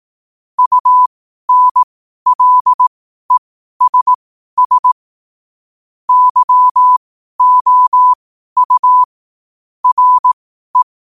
Morse code, heard as a keyed tone, about 18 wpm, spaced as for 12 wpm, 1000 Hz. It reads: UNLESS YOU RE
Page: 12 wpm morse code quotes